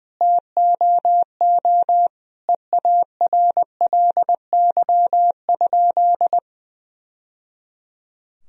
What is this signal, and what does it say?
Morse code, heard as a keyed tone, about 20 wpm, 700 Hz. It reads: TOO EARLY?
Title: Sentences from Top 1000 Words - Review Part 2 - 20wpm